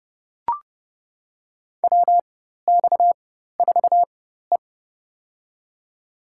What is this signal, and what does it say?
Morse code, heard as a keyed tone, about 29 wpm, spaced as for 14 wpm, 700 Hz. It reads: WX4E
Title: USA Call Signs - 4 Characters - 4x Character Spacing - 30wpm